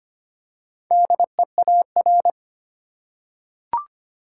Morse code, heard as a keyed tone, about 25 wpm, 700 Hz. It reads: DEAR
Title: Four Letter Words - 25wpm